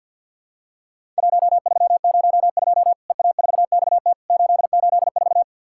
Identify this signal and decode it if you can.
Morse code, heard as a keyed tone, about 50 wpm, 700 Hz. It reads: 1302 EA4XT 884
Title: Worked All Europe DX Contest - QTC Exchange - 50wpm